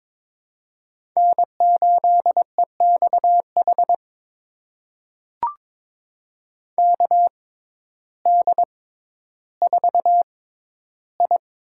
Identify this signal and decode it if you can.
Morse code, heard as a keyed tone, about 23 wpm, spaced as for 12 wpm, 700 Hz. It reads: N8EXH KD4I
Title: USA Call Signs - 5 Characters - 6x Character Spacing - 22wpm